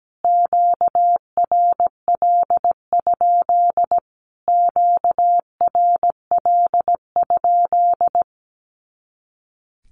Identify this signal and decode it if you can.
Morse code, heard as a keyed tone, about 17 wpm, 700 Hz. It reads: QRL? QRL?
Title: Q-Codes Repeated 3x 17wpm